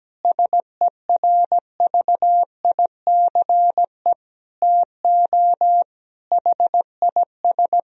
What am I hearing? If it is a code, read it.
Morse code, heard as a keyed tone, about 17 wpm, 700 Hz. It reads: SERVICE TO HIS
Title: Sentences from Top 1000 Words - Review Part 4 - 17wpm